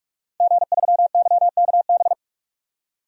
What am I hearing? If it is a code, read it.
Morse code, heard as a keyed tone, about 45 wpm, 700 Hz. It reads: G3YKB
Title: DXCC-100 Call Signs - 5 Characters - 45wpm